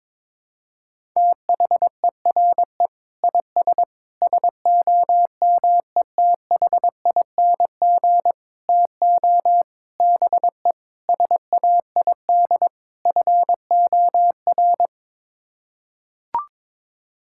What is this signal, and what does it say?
Morse code, heard as a keyed tone, about 22 wpm, 700 Hz. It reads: THERE IS SOMETHING TO BE SAID FOR
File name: practice / 22wpm / Sets of 7 Words - 22wpm